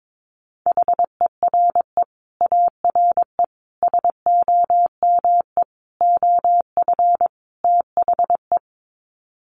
Morse code, heard as a keyed tone, about 22 wpm, 700 Hz. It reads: HERE ARE SOME OF THE
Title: Sets of 5 Words Encore 22wpm